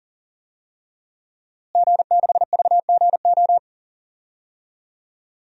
Morse code, heard as a keyed tone, about 40 wpm, 700 Hz. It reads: G6VGO
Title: DXCC-100 Call Signs - 5 Characters - Repeated 2x - 40wpm